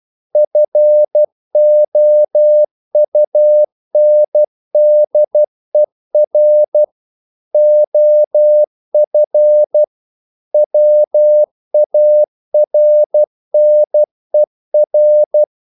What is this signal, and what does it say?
Morse code, heard as a keyed tone, about 12 wpm, 600 Hz. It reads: FOUNDER OF WARNER
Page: English quotations in CW (telegraphy - morse)